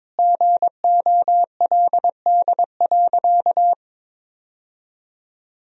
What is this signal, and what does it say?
Morse code, heard as a keyed tone, about 22 wpm, 700 Hz. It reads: GOLD.
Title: Sentences from Top 1000 Words - Review Part 3 - 22wpm